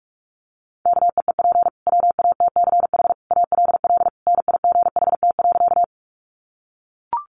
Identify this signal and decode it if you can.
Morse code, heard as a keyed tone, about 44 wpm, 700 Hz. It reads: KEEP WATCH ALL NIGHT.
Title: Sentences from Top 300 Words 45wpm